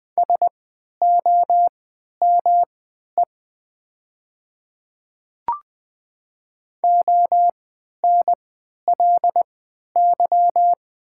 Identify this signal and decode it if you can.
Morse code, heard as a keyed tone, about 20 wpm, spaced as for 11 wpm, 700 Hz. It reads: SOME ONLY
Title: 3x Character Spacing - Top 500 Words - 20wpm